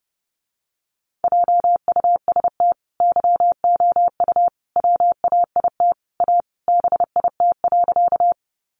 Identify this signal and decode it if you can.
Morse code, heard as a keyed tone, about 30 wpm, 700 Hz. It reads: JUST YOU WAIT A BIT.
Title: Sentences from Top 900 Words 30wpm